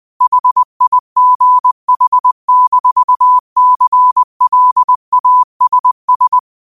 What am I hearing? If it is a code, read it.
Morse code, heard as a keyed tone, about 20 wpm, 1000 Hz. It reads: HIGH-CLASS